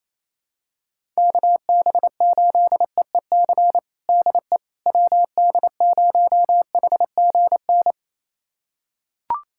Morse code, heard as a keyed tone, about 28 wpm, 700 Hz. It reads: KB8EEC DE WD0HGN
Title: USA Callsign DE Callsign - 6 Characters - 28wpm